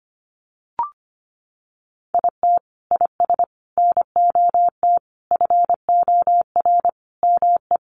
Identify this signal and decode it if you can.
Morse code, heard as a keyed tone, about 25 wpm, 700 Hz. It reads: IT IS NOT FOR ME
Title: Sentences from Top 1500 Words 25wpm